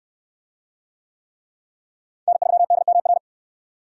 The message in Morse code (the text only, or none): N3DMR